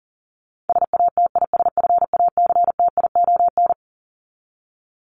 Morse code, heard as a keyed tone, about 40 wpm, 700 Hz. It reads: SATISFACTION
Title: Top 1751-2000 Words 40wpm